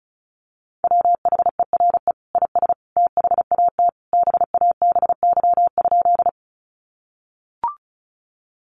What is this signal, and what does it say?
Morse code, heard as a keyed tone, about 34 wpm, 700 Hz. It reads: WHERE IS THAT BABY?
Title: Sentences from Top 1300 Words 35wpm